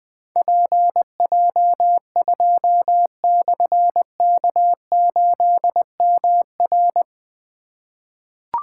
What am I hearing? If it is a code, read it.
Morse code, heard as a keyed tone, about 21 wpm, 700 Hz. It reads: PJ2/K8MR
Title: Difficult Call Signs 20wpm